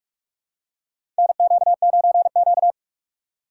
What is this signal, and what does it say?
Morse code, heard as a keyed tone, about 46 wpm, 700 Hz. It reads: NQ9Q
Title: USA Call Signs - 4 Characters - 4x Character Spacing - 45wpm